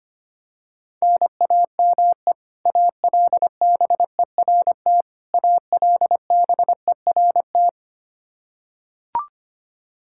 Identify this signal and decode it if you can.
Morse code, heard as a keyed tone, about 25 wpm, 700 Hz. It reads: NAME ALBERT ALBERT
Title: Practice QSO - Exchange 1 - Beginning - 25wpm